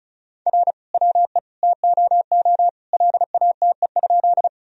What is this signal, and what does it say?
Morse code, heard as a keyed tone, about 36 wpm, 700 Hz. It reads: R WE TOO LATE?
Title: Sentences from Top 300 Words and Abbreviations 35wpm